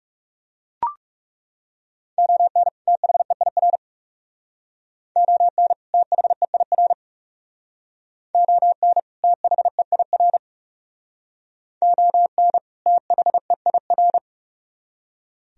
Morse code, heard as a keyed tone, about 36 wpm, 700 Hz. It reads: ON THEIR ON THEIR ON THEIR ON THEIR
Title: Speed-Racing - Sets of 2 Words 30wpm